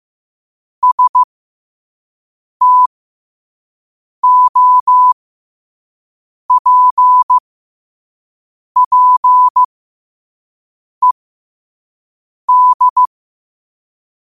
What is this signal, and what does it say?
Morse code, heard as a keyed tone, about 15 wpm, spaced as for 5 wpm, 1000 Hz. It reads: STOPPED
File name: qotdfiles / 5wpm